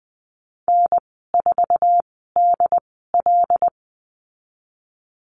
Morse code, heard as a keyed tone, about 20 wpm, spaced as for 15 wpm, 700 Hz. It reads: N4DL
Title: 076 Lesson 13g - Letter L - Call Signs 20wpm